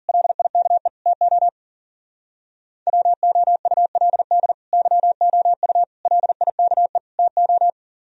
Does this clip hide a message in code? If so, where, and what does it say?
Yes: Morse code from 0.1 to 7.7 s, heard as a keyed tone, about 51 wpm, 700 Hz. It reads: LIKE TO WOULD YOU LIKE TO